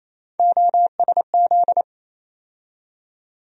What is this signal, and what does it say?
Morse code, heard as a keyed tone, about 28 wpm, 700 Hz. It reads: OSZ
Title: USA Call Sign Postfixes - 3 Letters 28wpm